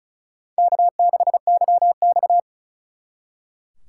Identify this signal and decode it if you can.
Morse code, heard as a keyed tone, about 43 wpm, 700 Hz. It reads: K6YX
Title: Speed Racing - USA Call Signs 35wpm